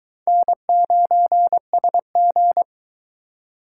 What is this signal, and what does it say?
Morse code, heard as a keyed tone, about 23 wpm, 700 Hz. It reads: N9SG